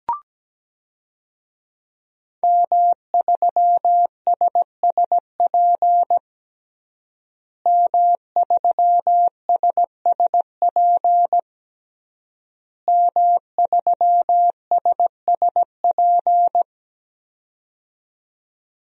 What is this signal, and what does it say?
Morse code, heard as a keyed tone, about 17 wpm, 700 Hz. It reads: M3SSP M3SSP M3SSP
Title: DXCC-100 Call Signs - 5 Characters - Repeated 3x - 17wpm